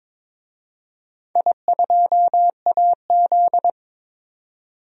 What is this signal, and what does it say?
Morse code, heard as a keyed tone, about 22 wpm, 700 Hz. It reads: I2AZ
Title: DXCC-100 Call Signs - 4 Characters - 22wpm